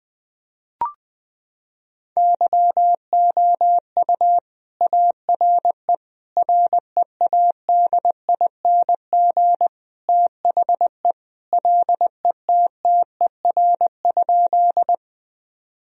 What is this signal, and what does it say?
Morse code, heard as a keyed tone, about 20 wpm, 700 Hz. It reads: YOU ARE READING THE LETTER?
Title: Sentences from Top 1000 Words - Review Part 3 - 20wpm